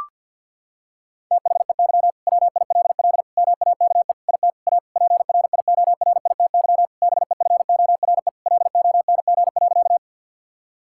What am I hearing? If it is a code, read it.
Morse code, heard as a keyed tone, about 50 wpm, 700 Hz. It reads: THEY WILL MAKE IT A PRIORITY BEFORE LONG.
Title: Sentences from Top 1600 Words 50wpm